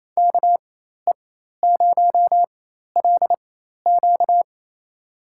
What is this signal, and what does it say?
Morse code, heard as a keyed tone, about 28 wpm, spaced as for 13 wpm, 700 Hz. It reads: KE0LQ
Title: USA Call Signs - 4x Character Spacing - 28wpm